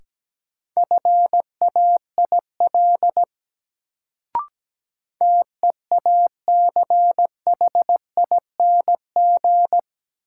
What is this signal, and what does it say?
Morse code, heard as a keyed tone, about 17 wpm, 700 Hz. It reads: FAIL TEACHING
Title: Sentences from Top 1100 Words - New Word Drill Alt - 17wpm